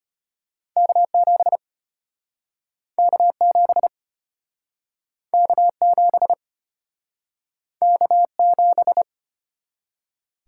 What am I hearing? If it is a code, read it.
Morse code, heard as a keyed tone, about 34 wpm, 700 Hz. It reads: K7 K7 K7 K7